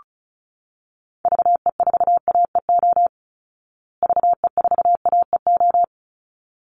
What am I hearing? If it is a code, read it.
Morse code, heard as a keyed tone, about 34 wpm, 700 Hz. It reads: VE4AEO VE4AEO
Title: Call Signs Repeated 3x - 35wpm